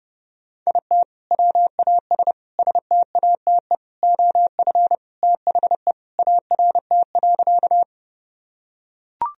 Morse code, heard as a keyed tone, about 30 wpm, 700 Hz. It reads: IT WAS STATE OF THE ART.
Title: Sentences from Top 1000 Words - Review Part 3 - 30wpm